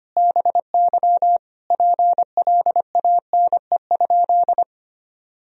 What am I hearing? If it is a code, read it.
Morse code, heard as a keyed tone, about 25 wpm, 700 Hz. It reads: BY PLANE?
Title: Sentences from Top 1000 Words - Review Part 3 - 25wpm